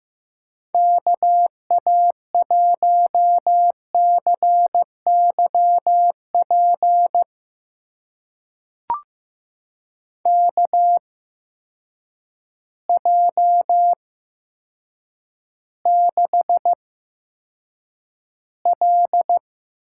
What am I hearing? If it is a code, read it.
Morse code, heard as a keyed tone, about 15 wpm, spaced as for 8 wpm, 700 Hz. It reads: KA1CYP KJ6L
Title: USA Call Signs - 6 Characters - 8x Character Spacing - 15wpm